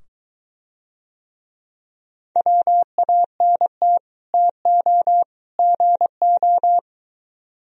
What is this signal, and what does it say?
Morse code, heard as a keyed tone, about 23 wpm, 700 Hz. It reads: WANT TO GO